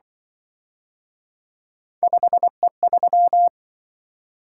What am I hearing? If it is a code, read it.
Morse code, heard as a keyed tone, about 23 wpm, 700 Hz. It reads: 5E3
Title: Speed-Racing - Three Letter-Number Combinations - 20wpm